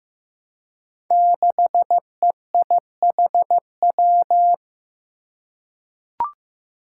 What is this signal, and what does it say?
Morse code, heard as a keyed tone, about 15 wpm, 700 Hz. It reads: 6EIHW
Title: Five Challenging Letter-Number Combinations 15wpm